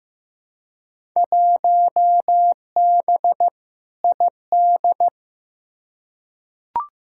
Morse code, heard as a keyed tone, about 15 wpm, 700 Hz. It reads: 1B ID
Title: Field Day Exchanges - 15wpm